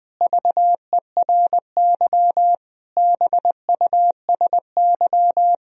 Morse code, heard as a keyed tone, about 21 wpm, 700 Hz. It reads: VERY BUSY
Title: Sentences from Top 1000 Words - Review Part 3 - 20wpm